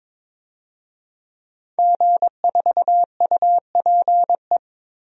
Morse code, heard as a keyed tone, about 22 wpm, 700 Hz. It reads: G4UPE
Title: DXCC-100 Call Signs - 5 Characters - 22wpm